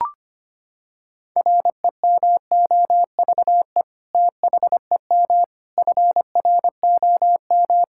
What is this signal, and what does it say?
Morse code, heard as a keyed tone, about 25 wpm, 700 Hz. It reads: REMOVE THEM FROM
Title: Sentences from Top 1000 Words - Review Part 5 - 25wpm